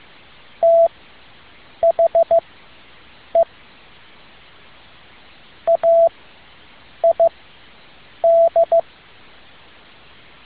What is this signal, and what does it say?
Morse code, heard as a keyed tone, about 15 wpm, spaced as for 7 wpm, 670 Hz. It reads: THE AID
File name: tin-upgrade / QSO 2013 02 12